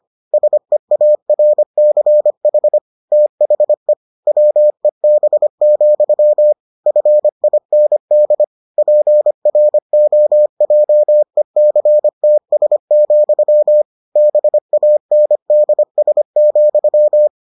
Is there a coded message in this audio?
Yes: Morse code from 0.3 to 17.3 s, heard as a keyed tone, about 25 wpm, 600 Hz. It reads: SEARCH THE WEB, FIND PROJECTS, BANDS,